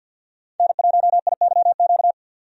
Morse code, heard as a keyed tone, about 52 wpm, 700 Hz. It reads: N1IYQ
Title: USA Call Signs - 5 Characters - 50wpm